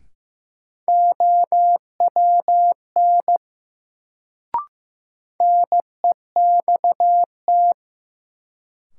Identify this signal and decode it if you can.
Morse code, heard as a keyed tone, about 15 wpm, 700 Hz. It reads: OWN NEXT
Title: Sentences from Top 300 Words - New Word Drill - 15wpm